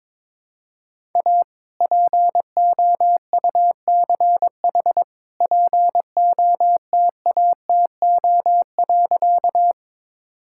Morse code, heard as a keyed tone, about 22 wpm, 700 Hz. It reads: A POUCH POTATO.
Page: morse-code-ninja-practice-22wpm